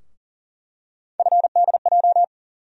Morse code, heard as a keyed tone, about 40 wpm, 700 Hz. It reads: FDJ
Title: USA Call Sign Postfixes - 3 Letters 40wpm